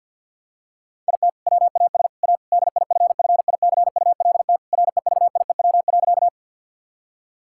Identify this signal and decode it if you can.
Morse code, heard as a keyed tone, about 50 wpm, 700 Hz. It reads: IT WAS A DIFFICULT REVIEW.
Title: Sentences from Top 1000 Words - Review Part 6 - 50wpm